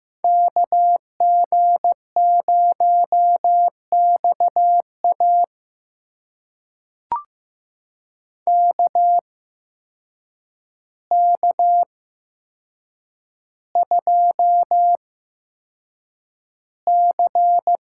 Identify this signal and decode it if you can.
Morse code, heard as a keyed tone, about 15 wpm, spaced as for 7 wpm, 700 Hz. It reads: KG0XA KK2C
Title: USA Call Signs - 5 Characters - 8x Character Spacing - 15wpm